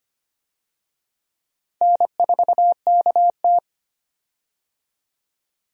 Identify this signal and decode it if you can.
Morse code, heard as a keyed tone, about 25 wpm, 700 Hz. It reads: N4KT
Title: USA Call Signs - 4 Characters - 25wpm